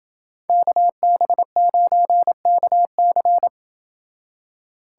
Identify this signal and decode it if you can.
Morse code, heard as a keyed tone, about 28 wpm, 700 Hz. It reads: KB9KC